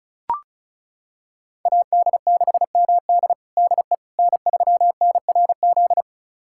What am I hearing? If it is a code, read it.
Morse code, heard as a keyed tone, about 36 wpm, 690 Hz. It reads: AD6MD DE N3NRZ
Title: USA Callsign DE Callsign - 5 Characters - 35wpm